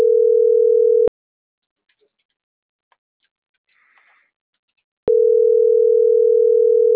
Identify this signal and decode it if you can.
Ringback tone (North America), 440 + 480 Hz.